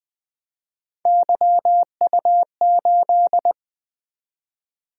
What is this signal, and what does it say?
Morse code, heard as a keyed tone, about 20 wpm, 700 Hz. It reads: YU8
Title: Three Letter-Number Combinations - 20wpm